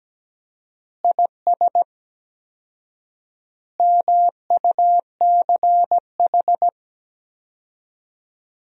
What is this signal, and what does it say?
Morse code, heard as a keyed tone, about 17 wpm, 700 Hz. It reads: IS MUCH
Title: 4x Word-Spacing - Sentences from Top 400 Words - 17wpm